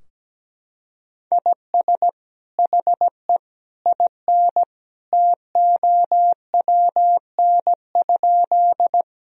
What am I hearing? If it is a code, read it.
Morse code, heard as a keyed tone, about 17 wpm, 700 Hz. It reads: IS HE IN TOWN?
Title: Sentences from Top 1000 Words - Review Part 2 - 17wpm